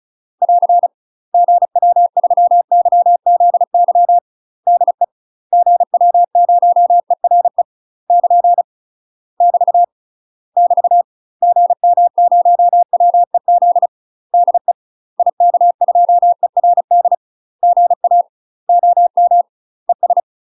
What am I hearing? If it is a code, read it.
Morse code, heard as a keyed tone, about 34 wpm, 700 Hz. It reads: <AR> GW3YZY DE GW0ERE <KN> <BT> <BT> GM0WEZ DE IK2ERD GA OM ES
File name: UK MASTER QSOs TRAINING / 35WPM / UK MASTER QSOs 0027